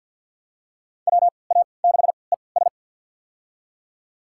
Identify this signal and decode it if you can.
Morse code, heard as a keyed tone, about 46 wpm, spaced as for 27 wpm, 700 Hz. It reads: WA6ES
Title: USA Call Signs - 3x Character Spacing - 50wpm